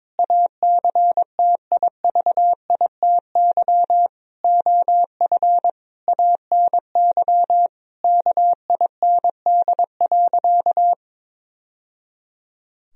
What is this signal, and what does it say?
Morse code, heard as a keyed tone, about 22 wpm, 700 Hz. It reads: ACTIVITY OF ANY KIND.
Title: Sentences from Top 1000 Words - Review Part 4 - 22wpm